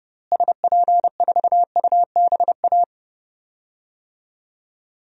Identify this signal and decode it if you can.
Morse code, heard as a keyed tone, about 30 wpm, 700 Hz. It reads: SP4UBA
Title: DXCC-100 Call Signs - 6 Characters - Repeated 2x - 30wpm